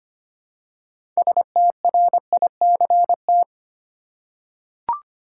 Morse code, heard as a keyed tone, about 25 wpm, 700 Hz. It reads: STRICT
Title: Six Letter Words - 25wpm